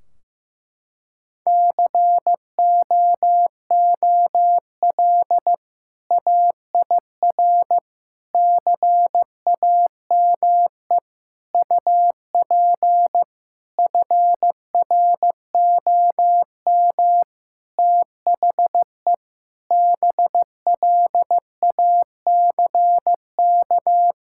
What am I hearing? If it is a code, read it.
Morse code, heard as a keyed tone, about 15 wpm, 700 Hz. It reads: COOL AIR CAME UP FROM THE BLACK
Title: Sentences from Top 1600 Words 15wpm